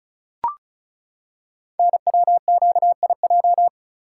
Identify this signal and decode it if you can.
Morse code, heard as a keyed tone, about 36 wpm, 700 Hz. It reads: NWQIJ
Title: Code Groups - 5 Letter:Numbers - 35wpm